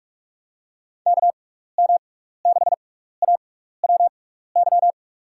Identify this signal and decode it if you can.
Morse code, heard as a keyed tone, about 46 wpm, spaced as for 16 wpm, 700 Hz. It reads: KM6AWY
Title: USA Call Signs - 6 Characters - 6x Character Spacing - 45wpm